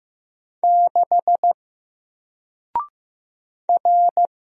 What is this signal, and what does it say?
Morse code, heard as a keyed tone, about 15 wpm, 700 Hz. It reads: 6 R
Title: Single Letter-Number Alt 15wpm